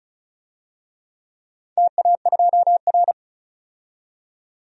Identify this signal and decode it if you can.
Morse code, heard as a keyed tone, about 36 wpm, 690 Hz. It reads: TA2R
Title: DXCC-100 Call Signs - 4 Characters - 35wpm